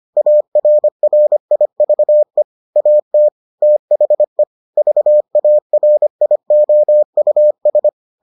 Morse code, heard as a keyed tone, about 25 wpm, 600 Hz. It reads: ARRIVE AT THE VARIOUS